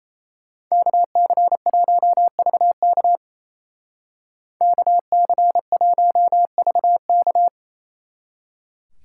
Morse code, heard as a keyed tone, about 32 wpm, 700 Hz. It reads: KC1VK KC1VK
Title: Speed Racing - USA 2x2 Call Signs 28wpm